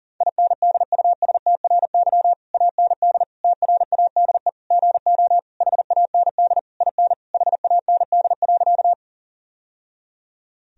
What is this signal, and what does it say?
Morse code, heard as a keyed tone, about 40 wpm, 700 Hz. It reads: INDUSTRY AND TRADE GO HAND IN HAND.
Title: Sentences from Top 1000 Words - Review Part 4 - 40wpm